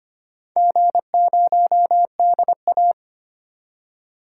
Morse code, heard as a keyed tone, about 25 wpm, 700 Hz. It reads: G0DA
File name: practice / 25wpm / Four Letter-Number Combinations - 25wpm